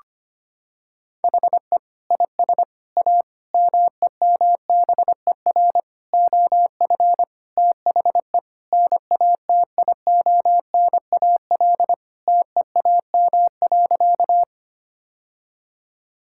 Morse code, heard as a keyed tone, about 25 wpm, 700 Hz. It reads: HE IS A MEMBER OF THE NATIONAL TEAM.
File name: practice / 25wpm / Sentences with 3-Word Phrases - 25wpm